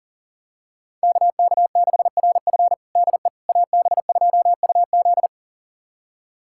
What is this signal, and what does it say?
Morse code, heard as a keyed tone, about 40 wpm, 700 Hz. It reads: KK6RF DE AB2UZ
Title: USA Callsign DE Callsign - 5 Characters - 40wpm